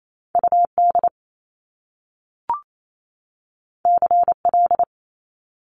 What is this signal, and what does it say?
Morse code, heard as a keyed tone, about 28 wpm, 700 Hz. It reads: UD CL
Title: Top Two Letter Combinations in Words 28wpm